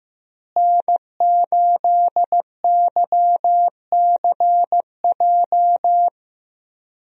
Morse code, heard as a keyed tone, about 15 wpm, 700 Hz. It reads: N8YCJ
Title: USA Call Signs - 5 Characters - Repeated 3x - 15wpm